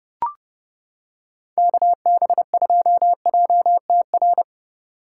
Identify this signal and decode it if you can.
Morse code, heard as a keyed tone, about 28 wpm, 700 Hz. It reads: KB2JTR